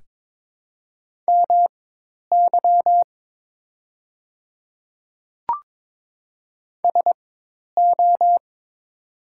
Morse code, heard as a keyed tone, about 22 wpm, spaced as for 10 wpm, 700 Hz. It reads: MY SO
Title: 4x Character Spacing - Top 100 Words - 22wpm